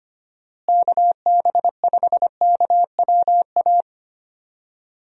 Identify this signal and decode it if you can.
Morse code, heard as a keyed tone, about 25 wpm, 700 Hz. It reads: KB5KWA
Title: USA Call Signs - 6 Characters - Repeated 3x - 25wpm